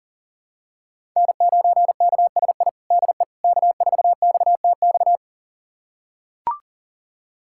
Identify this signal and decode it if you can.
Morse code, heard as a keyed tone, about 40 wpm, 700 Hz. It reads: N9KSI DE K4XTX
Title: USA Callsign DE Callsign - 5 Characters - 40wpm